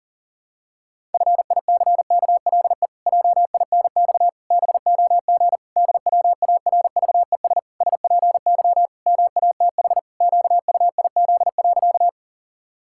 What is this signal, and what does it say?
Morse code, heard as a keyed tone, about 40 wpm, 700 Hz. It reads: FICKLE JINX BOG DWARVES SPY MATH QUIZ.
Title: Pangram Challenge v2 - 40wpm